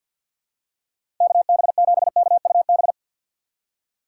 Morse code, heard as a keyed tone, about 52 wpm, 700 Hz. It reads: KB7KUB
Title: USA Call Signs - 6 Characters - 3x Character Spacing - 50wpm